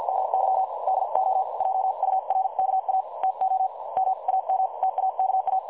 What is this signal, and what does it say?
Morse code, heard as a keyed tone, about 50 wpm, spaced as for 43 wpm, 760 Hz. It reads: MY DO JURKA TO MAM TAKA
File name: dlazaawansowanych / nagraniazpasma / websdr recording 3565.0kHz